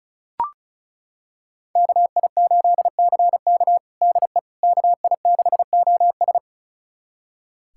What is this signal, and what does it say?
Morse code, heard as a keyed tone, about 34 wpm, 700 Hz. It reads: KI8CK DE KI6OS
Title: USA Callsign DE Callsign - 5 Characters - 35wpm